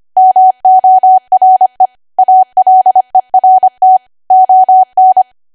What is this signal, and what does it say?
Morse code, heard as a keyed tone, about 25 wpm, 750 Hz. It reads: MORE ALERT ON
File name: cwa / W1AW-Level3 / 25WPM